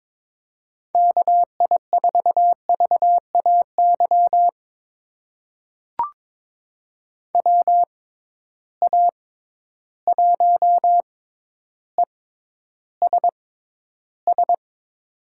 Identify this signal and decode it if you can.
Morse code, heard as a keyed tone, about 22 wpm, spaced as for 11 wpm, 700 Hz. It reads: KI4VAY WA1ESS
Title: USA Call Signs - 6 Characters - 6x Character Spacing - 22wpm